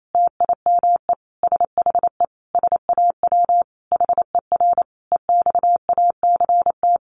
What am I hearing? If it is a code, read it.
Morse code, heard as a keyed tone, about 28 wpm, 700 Hz. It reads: TIME SHE SAW HER EXACT
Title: Sentences from Top 1300 Words 28wpm